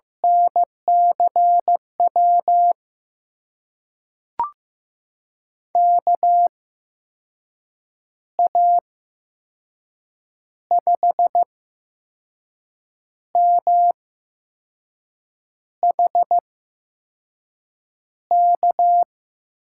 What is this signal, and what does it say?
Morse code, heard as a keyed tone, about 15 wpm, spaced as for 5 wpm, 700 Hz. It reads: NCW KA5MHK
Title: USA Call Signs - 6 Characters - 8x Character Spacing - 15wpm